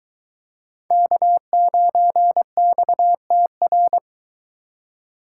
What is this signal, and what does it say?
Morse code, heard as a keyed tone, about 21 wpm, 700 Hz. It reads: K9XTR